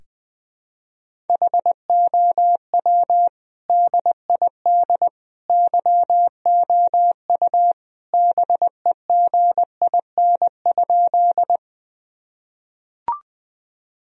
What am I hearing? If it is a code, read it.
Morse code, heard as a keyed tone, about 20 wpm, 700 Hz. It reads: HOW DID YOU BEGIN?
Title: Sentences from Top 1000 Words - Review Part 1 - 20wpm